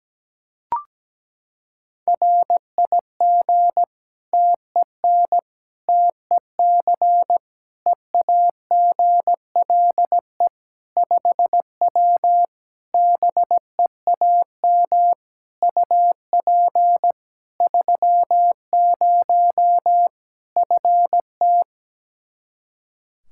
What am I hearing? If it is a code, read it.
Morse code, heard as a keyed tone, about 17 wpm, 700 Hz. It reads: RIG TEN TEC EAGLE 5W BEAM UP 30 FT
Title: Practice QSO - Exchange 2 - Info - 17wpm